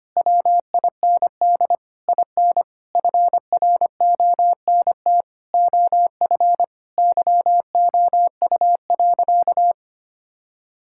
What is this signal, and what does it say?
Morse code, heard as a keyed tone, about 25 wpm, 700 Hz. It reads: WIND IN FRONT OF YOU.
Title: Sentences from Top 1000 Words - Review Part 6 - 25wpm